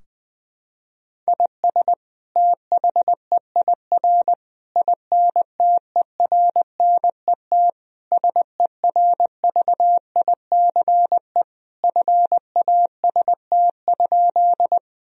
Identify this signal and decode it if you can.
Morse code, heard as a keyed tone, about 20 wpm, 700 Hz. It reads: IS THEIR INTERNET SERVICE FAST?
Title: Sentences from Top 1000 Words - Review Part 4 - 20wpm